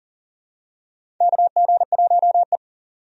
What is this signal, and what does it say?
Morse code, heard as a keyed tone, about 40 wpm, 700 Hz. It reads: KG1E
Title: USA Call Signs - 4 Characters - 40wpm